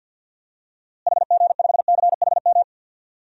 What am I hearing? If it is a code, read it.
Morse code, heard as a keyed tone, about 52 wpm, 700 Hz. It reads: HG5CHM
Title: DXCC-100 Call Signs - 6 Characters - Repeated 3x - 50wpm